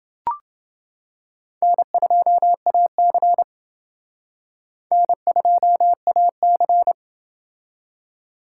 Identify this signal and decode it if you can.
Morse code, heard as a keyed tone, about 28 wpm, 700 Hz. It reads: N2AC N2AC